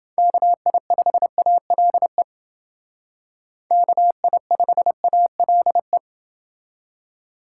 Morse code, heard as a keyed tone, about 28 wpm, 700 Hz. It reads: KI5ALE KI5ALE